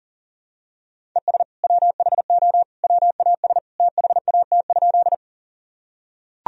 Morse code, heard as a keyed tone, about 40 wpm, 700 Hz. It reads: ES WHO WAS THAT?
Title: Sentences from Top 100 Words and Abbreviations 40wpm